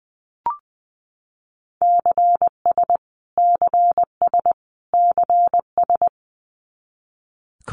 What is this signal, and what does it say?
Morse code, heard as a keyed tone, about 20 wpm, 700 Hz. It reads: CS CS CS
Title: Abbreviations Repeated 3x - 20wpm